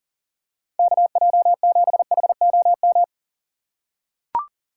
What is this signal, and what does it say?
Morse code, heard as a keyed tone, about 40 wpm, 700 Hz. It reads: KJ7HOM
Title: USA Call Signs - 6 Characters - 6x Character Spacing - 40wpm